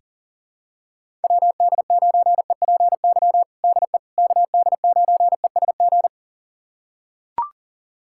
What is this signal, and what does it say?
Morse code, heard as a keyed tone, about 37 wpm, 700 Hz. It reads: WD9EPY DE KD9ESG
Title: USA Callsign DE Callsign 40wpm